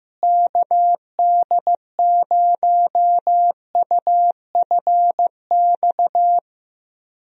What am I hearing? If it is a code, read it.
Morse code, heard as a keyed tone, about 15 wpm, 700 Hz. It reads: KD0UFX
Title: USA Call Signs - 6 Characters - Repeated 2x - 15wpm